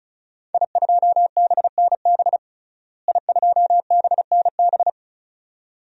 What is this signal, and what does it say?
Morse code, heard as a keyed tone, about 36 wpm, 700 Hz. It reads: I2BNB I2BNB
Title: DXCC-100 Call Signs - 5 Characters - Repeated 2x - 35wpm